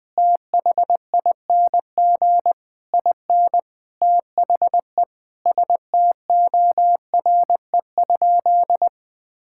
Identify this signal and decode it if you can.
Morse code, heard as a keyed tone, about 19 wpm, 700 Hz. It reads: THING IN THE STORE?
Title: Sentences from Top 1000 Words - Review Part 6 - 20wpm